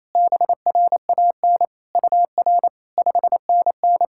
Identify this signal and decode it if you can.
Morse code, heard as a keyed tone, about 27 wpm, 700 Hz. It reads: BRAN UR 5NN
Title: SOTA QSO Practice 28wpm